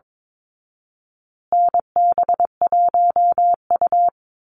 Morse code, heard as a keyed tone, about 22 wpm, 700 Hz. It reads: NB1U
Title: Call Signs Repeated 2x - 22wpm